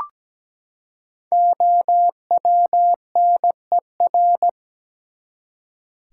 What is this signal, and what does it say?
Morse code, heard as a keyed tone, about 17 wpm, 700 Hz. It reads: OWNER
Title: Five Letter Words - 17wpm